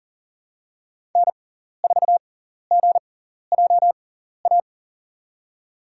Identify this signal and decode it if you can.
Morse code, heard as a keyed tone, about 38 wpm, spaced as for 14 wpm, 700 Hz. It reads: N4GJA
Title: USA Call Signs - 5 Characters - 6x Character Spacing - 40wpm